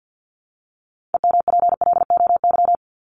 Morse code, heard as a keyed tone, about 48 wpm, 700 Hz. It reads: EMPLOY
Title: Top 1751-2000 Words 50wpm